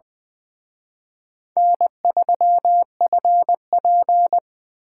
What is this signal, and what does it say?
Morse code, heard as a keyed tone, about 24 wpm, 700 Hz. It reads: N3FP
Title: Speed Racing - USA 1x2 Call Signs 20wpm